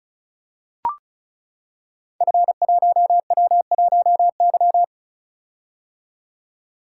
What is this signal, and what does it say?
Morse code, heard as a keyed tone, about 36 wpm, 690 Hz. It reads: F1W1Y
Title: Code Groups - 5 Letter:Numbers - 35wpm